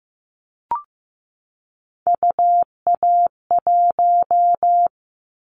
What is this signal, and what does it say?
Morse code, heard as a keyed tone, about 15 wpm, 700 Hz. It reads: UA1
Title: Callsign Prefixes for Top 50 DXCC - 15wpm